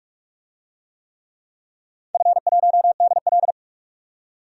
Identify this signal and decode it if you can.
Morse code, heard as a keyed tone, about 46 wpm, 700 Hz. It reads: F1DL
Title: DXCC-100 Call Signs - 4 Characters - 45wpm